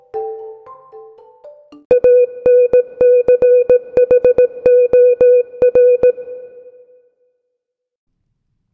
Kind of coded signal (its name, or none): Morse code